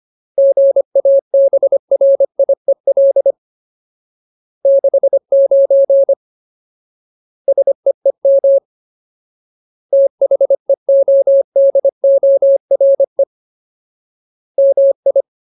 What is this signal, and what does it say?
Morse code, heard as a keyed tone, about 25 wpm, 550 Hz. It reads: GABRIEL 69 SEEM THEODORE MI
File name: Ultimate Word List / 25WPM Ultimate